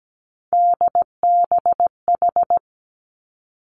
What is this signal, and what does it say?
Morse code, heard as a keyed tone, about 17 wpm, 700 Hz. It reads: DBH
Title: Three Challenging Letter-Number Combinations 17wpm